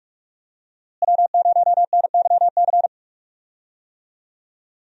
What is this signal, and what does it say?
Morse code, heard as a keyed tone, about 41 wpm, 700 Hz. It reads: W0NYC
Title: USA Call Signs - 5 Characters - Repeated 2x - 45wpm